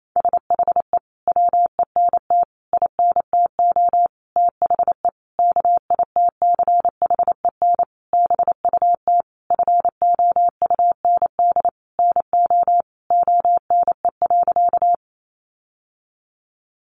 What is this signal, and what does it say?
Morse code, heard as a keyed tone, about 28 wpm, 700 Hz. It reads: SHE WENT INTO THE KITCHEN BUT FOUND NO ONE.
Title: Sentences from Top 1400 Words 28wpm